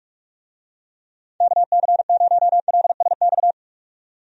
Morse code, heard as a keyed tone, about 46 wpm, 700 Hz. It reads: KC0LSX
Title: USA Call Signs - 6 Characters - 45wpm